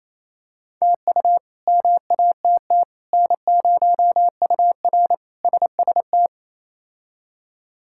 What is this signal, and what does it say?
Morse code, heard as a keyed tone, about 27 wpm, 700 Hz. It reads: TU MATT N0UR SST
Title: K1USN SST QSO Practice - 28wpm